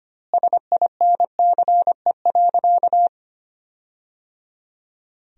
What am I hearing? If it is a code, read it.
Morse code, heard as a keyed tone, about 25 wpm, 700 Hz. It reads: SINCE.
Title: 2x Word-Spacing - Sentences from Top 400 Words - 25wpm